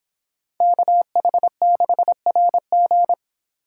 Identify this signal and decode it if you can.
Morse code, heard as a keyed tone, about 31 wpm, 700 Hz. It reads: KH6RG